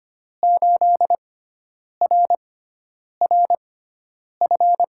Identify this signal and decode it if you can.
Morse code, heard as a keyed tone, about 25 wpm, spaced as for 9 wpm, 700 Hz. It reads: 8RRF
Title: USA Call Signs - 5 Characters - 6x Character Spacing - 25wpm